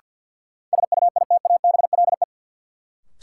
Morse code, heard as a keyed tone, about 46 wpm, 700 Hz. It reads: SUITABLE
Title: Sentences from Top 1400 Words - New Word Drill Alt - 50wpm